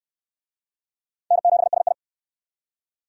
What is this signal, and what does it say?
Morse code, heard as a keyed tone, about 68 wpm, 700 Hz. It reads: N7SI